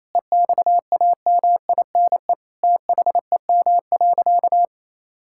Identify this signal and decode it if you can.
Morse code, heard as a keyed tone, about 28 wpm, 700 Hz. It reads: EXAMINE THEM.
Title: Sentences from Top 1600 Words 28wpm